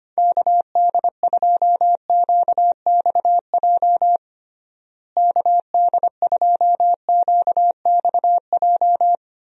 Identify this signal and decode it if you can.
Morse code, heard as a keyed tone, about 25 wpm, 700 Hz. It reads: KD2QXJ KD2QXJ
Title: USA Call Signs - 6 Characters - Repeated 2x - 25wpm